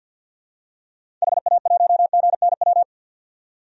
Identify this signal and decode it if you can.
Morse code, heard as a keyed tone, about 50 wpm, 700 Hz. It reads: HA1GNW
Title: DXCC-100 Call Signs - 6 Characters - 50wpm